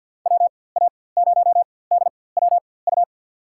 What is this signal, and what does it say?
Morse code, heard as a keyed tone, about 52 wpm, spaced as for 23 wpm, 700 Hz. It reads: WA0DWU
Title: USA Call Signs - 6 Characters - 4x Character Spacing - 50wpm